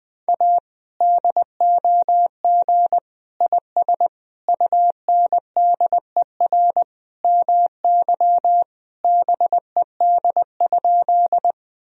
Morse code, heard as a keyed tone, about 20 wpm, 700 Hz. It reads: A DOG IS UNDER MY BED?
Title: Sentences from Top 1000 Words - Review Part 2 - 20wpm